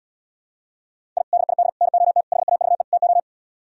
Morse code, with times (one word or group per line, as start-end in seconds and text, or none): 1.2-1.2 I
1.3-1.7 DID
1.8-2.2 NOT
2.3-2.8 HAVE
2.9-3.2 TO